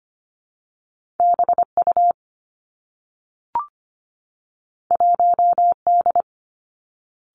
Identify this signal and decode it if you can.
Morse code, heard as a keyed tone, about 25 wpm, 700 Hz. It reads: B U 1 D
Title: Two Challenging Letter-Number Combinations 25wpm